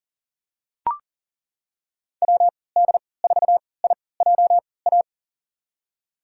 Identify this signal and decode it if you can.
Morse code, heard as a keyed tone, about 40 wpm, spaced as for 23 wpm, 700 Hz. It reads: WD4IJA
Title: USA Call Signs - 6 Characters - 3x Character Spacing - 40wpm